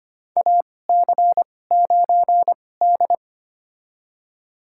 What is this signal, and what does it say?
Morse code, heard as a keyed tone, about 25 wpm, spaced as for 18 wpm, 700 Hz. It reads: AC9D
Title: USA Call Signs - 4 Characters - 2x Character Spacing - 25wpm